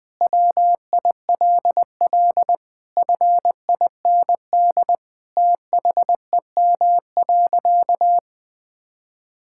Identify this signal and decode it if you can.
Morse code, heard as a keyed tone, about 20 wpm, 700 Hz. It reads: WILL FIND THEM.
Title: Sentences from Top 1000 Words - Review Part 6 - 20wpm